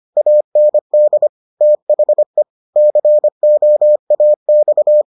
Morse code, heard as a keyed tone, about 25 wpm, 600 Hz. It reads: AND THE COAX